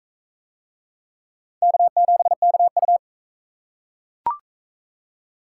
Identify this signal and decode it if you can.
Morse code, heard as a keyed tone, about 42 wpm, 700 Hz. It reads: K7KU